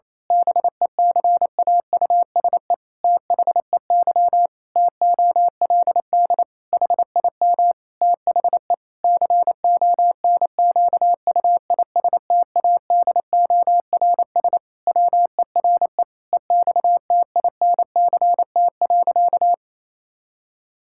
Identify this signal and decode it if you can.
Morse code, heard as a keyed tone, about 27 wpm, 700 Hz. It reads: BECAUSE THEY TOLD HIM THE CONQUISTADORS WERE EXTINCT.
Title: Pangram Challenge v2 - 28wpm